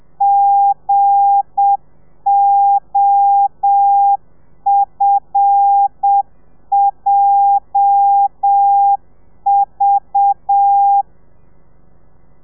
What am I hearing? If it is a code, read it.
Morse code, heard as a keyed tone, about 7 wpm, 800 Hz. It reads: GOFJV
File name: telegrafia / i2tfj / Lez8 30